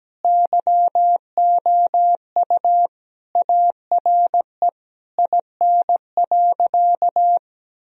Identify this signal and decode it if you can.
Morse code, heard as a keyed tone, about 17 wpm, 700 Hz. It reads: YOU ARE IN.
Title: Sentences from Top 1000 Words - Review Part 2 - 17wpm